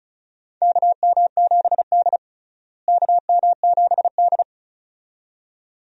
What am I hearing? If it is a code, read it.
Morse code, heard as a keyed tone, about 34 wpm, 700 Hz. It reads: KM7D KM7D
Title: USA Call Signs - 4 Characters - Repeated 2x - 35wpm